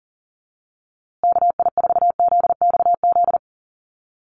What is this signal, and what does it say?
Morse code, heard as a keyed tone, about 40 wpm, 700 Hz. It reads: KI4ZXZ
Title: USA Call Signs Repeated 2x 40wpm